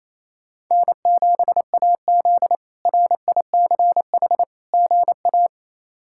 Morse code, heard as a keyed tone, about 28 wpm, 700 Hz. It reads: N7AZ RICH GA
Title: K1USN SST QSO Practice - 28wpm